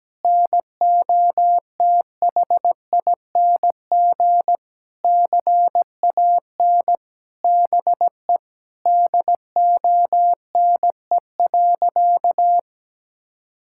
Morse code, heard as a keyed tone, about 17 wpm, 700 Hz. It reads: NOTHING CAN BE DONE.
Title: Sentences from Top 1600 Words 17wpm